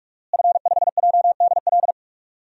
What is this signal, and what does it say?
Morse code, heard as a keyed tone, about 41 wpm, 700 Hz. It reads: F5JDL
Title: DXCC-100 Call Signs - 5 Characters - Repeated 2x - 45wpm